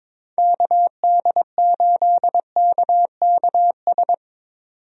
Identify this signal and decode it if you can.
Morse code, heard as a keyed tone, about 22 wpm, 700 Hz. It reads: KD8KKS
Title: USA Call Signs - 6 Characters - 22wpm